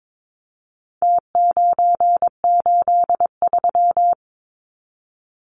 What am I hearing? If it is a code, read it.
Morse code, heard as a keyed tone, about 22 wpm, 700 Hz. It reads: T983
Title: Numbers - Contest Style - 4-Digit 22wpm